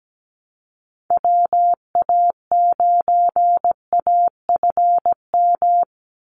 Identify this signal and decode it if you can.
Morse code, heard as a keyed tone, about 17 wpm, 700 Hz. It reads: WA9AFM
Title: USA Call Signs 17wpm